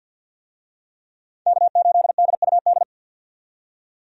Morse code, heard as a keyed tone, about 48 wpm, 700 Hz. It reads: K8DUD
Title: USA Call Signs - 5 Characters - 50wpm